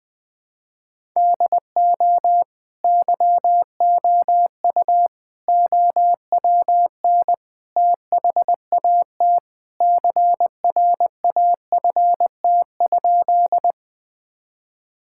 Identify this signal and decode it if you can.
Morse code, heard as a keyed tone, about 20 wpm, 700 Hz. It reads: DO YOU OWN THAT CRAFT?
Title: Sentences from Top 1000 Words - Review Part 6 - 20wpm